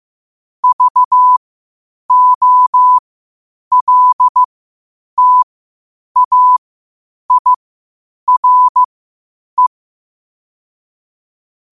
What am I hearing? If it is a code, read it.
Morse code, heard as a keyed tone, about 15 wpm, spaced as for 8 wpm, 1000 Hz. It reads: VOLTAIRE
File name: qotdfiles / 8wpm